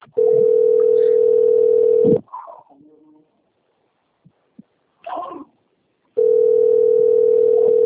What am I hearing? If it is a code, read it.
Ringback tone (North America), 440 + 480 Hz.
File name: out- - - - - .19907